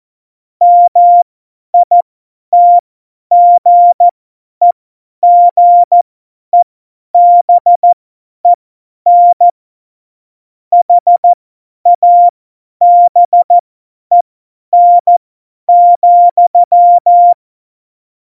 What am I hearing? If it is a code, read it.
Morse code, heard as a keyed tone, about 14 wpm, spaced as for 10 wpm, 700 Hz. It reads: MITGEGEBEN HABEN,
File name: Morsekurs / Witze / Witz 13. Oktober 2020, 50 F70